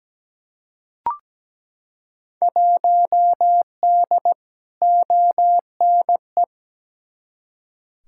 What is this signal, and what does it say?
Morse code, heard as a keyed tone, about 17 wpm, 700 Hz. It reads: 1D ONE
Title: Field Day Exchanges - 17wpm